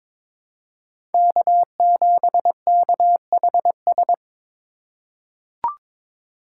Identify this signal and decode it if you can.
Morse code, heard as a keyed tone, about 22 wpm, 700 Hz. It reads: K7KHS
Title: USA Call Signs - 5 Characters - 2x Character Spacing - 22wpm